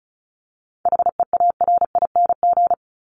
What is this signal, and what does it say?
Morse code, heard as a keyed tone, about 34 wpm, 700 Hz. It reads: HEARING
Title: ING Suffix - Top 100 Words with Base Words 35wpm